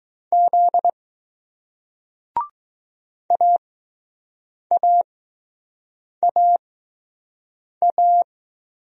Morse code, heard as a keyed tone, about 20 wpm, spaced as for 6 wpm, 700 Hz. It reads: Z AAAA